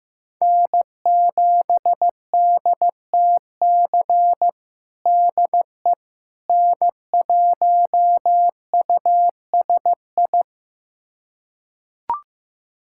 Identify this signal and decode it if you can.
Morse code, heard as a keyed tone, about 15 wpm, 700 Hz. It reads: N7DTC DE N1USI
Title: USA Callsign DE Callsign - 5 Characters - 15wpm